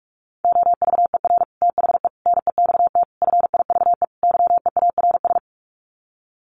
Morse code, heard as a keyed tone, about 44 wpm, 700 Hz. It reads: OVER THE NEXT FIVE YEARS
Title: Sets of 5 Words 45wpm